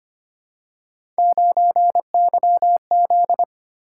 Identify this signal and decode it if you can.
Morse code, heard as a keyed tone, about 25 wpm, 700 Hz. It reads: 9YZ
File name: Three Letter-Number Combinations - 25wpm